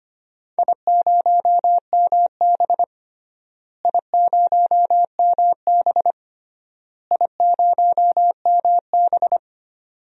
Morse code, heard as a keyed tone, about 25 wpm, 700 Hz. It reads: I0MB I0MB I0MB
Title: DXCC-100 Call Signs - 4 Characters - Repeated 3x - 25wpm